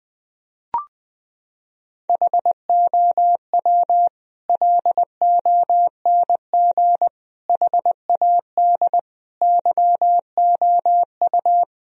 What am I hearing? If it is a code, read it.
Morse code, heard as a keyed tone, about 19 wpm, 700 Hz. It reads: HOW LONG HAD YOU
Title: Sentences from Top 1000 Words - Review Part 3 - 20wpm